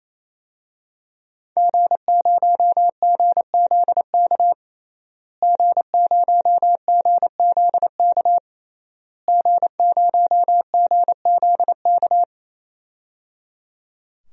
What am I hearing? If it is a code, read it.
Morse code, heard as a keyed tone, about 28 wpm, 700 Hz. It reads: G0GZK G0GZK G0GZK
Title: DXCC-100 Call Signs - 5 Characters - Repeated 3x - 28wpm